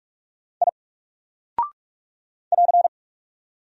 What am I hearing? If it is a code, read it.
Morse code, heard as a keyed tone, about 44 wpm, 700 Hz. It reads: I <AR>
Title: Single Character (Letter:Number:Punctuation:Prosign) 45wpm